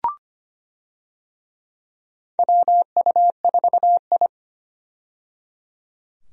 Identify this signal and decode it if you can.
Morse code, heard as a keyed tone, about 25 wpm, 700 Hz. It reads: WU4I
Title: USA Call Signs - 4 Characters - 25wpm